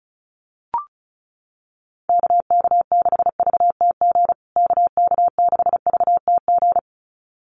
Morse code, heard as a keyed tone, about 34 wpm, 700 Hz. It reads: KK6VTG KK6VTG
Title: USA Call Signs Repeated 2x 35wpm